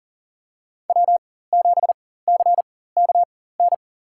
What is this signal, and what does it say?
Morse code, heard as a keyed tone, about 40 wpm, spaced as for 19 wpm, 700 Hz. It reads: W7CKN
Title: USA Call Signs - 4x Character Spacing - 40wpm